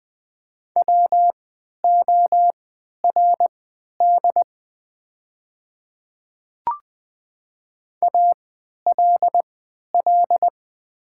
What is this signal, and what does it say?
Morse code, heard as a keyed tone, about 20 wpm, spaced as for 11 wpm, 700 Hz. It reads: WORD ALL
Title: 3x Character Spacing - Top 100 Words - 20wpm